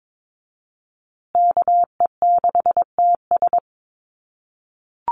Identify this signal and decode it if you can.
Morse code, heard as a keyed tone, about 22 wpm, 700 Hz. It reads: KE6TS
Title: USA Call Signs Repeated 3x 22wpm